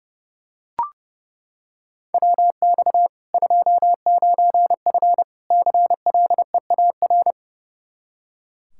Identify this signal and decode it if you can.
Morse code, heard as a keyed tone, about 30 wpm, 700 Hz. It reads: WX 29F CLEAR
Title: Practice QSO - Exchange 2 - Info - 30wpm